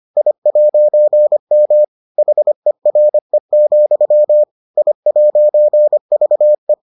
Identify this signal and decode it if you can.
Morse code, heard as a keyed tone, about 25 wpm, 600 Hz. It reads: I'M HERE, I'VE